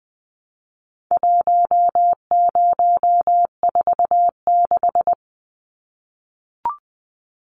Morse code, heard as a keyed tone, about 20 wpm, 700 Hz. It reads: 1046
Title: Numbers - 4-Digit 20wpm